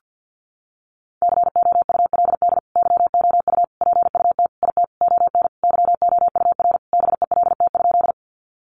Morse code, heard as a keyed tone, about 50 wpm, 700 Hz. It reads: COULD YOU PUT IT ON YOUR BELT?
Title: Sentences from Top 1400 Words 50wpm